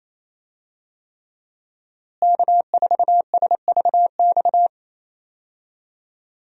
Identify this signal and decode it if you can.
Morse code, heard as a keyed tone, about 27 wpm, 700 Hz. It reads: K4SVX
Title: USA Call Signs - 5 Characters - Repeated 3x - 28wpm